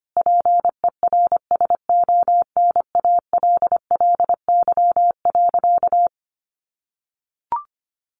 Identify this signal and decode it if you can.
Morse code, heard as a keyed tone, about 25 wpm, 700 Hz. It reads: PERSONALLY.
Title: Sentences from Top 1400 Words 25wpm